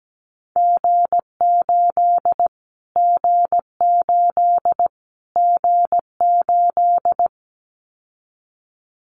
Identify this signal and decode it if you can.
Morse code, heard as a keyed tone, about 17 wpm, 700 Hz. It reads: G8 G8 G8
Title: Callsign Prefixes for Top 25 DXCC - Repeated 3x - 17wpm